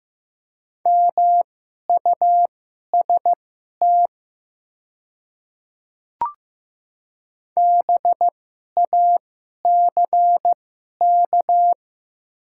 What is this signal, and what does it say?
Morse code, heard as a keyed tone, about 15 wpm, spaced as for 11 wpm, 700 Hz. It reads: MUST BACK
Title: 2x Character Spacing - Top 200 Words - 15wpm